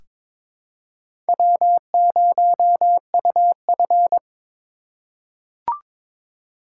Morse code, heard as a keyed tone, about 22 wpm, 700 Hz. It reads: W0UF
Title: K1USN SST QSO Practice - 22wpm